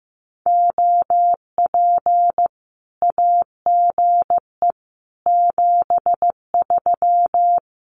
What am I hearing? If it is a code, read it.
Morse code, heard as a keyed tone, about 15 wpm, 700 Hz. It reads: OP AGE 73
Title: Common QSO Phrases 15wpm